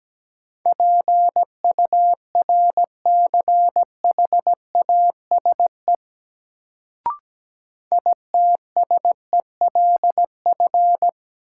Morse code, heard as a keyed tone, about 17 wpm, 700 Hz. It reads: PURCHASE ITSELF
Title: Sentences from Top 800 Words - New Word Drill - 17wpm